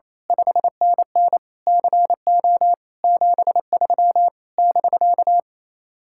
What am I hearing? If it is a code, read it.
Morse code, heard as a keyed tone, about 26 wpm, 700 Hz. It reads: 5NN CO 73 <BK>
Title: Speed-Racing Common QSO Phrases - 28wpm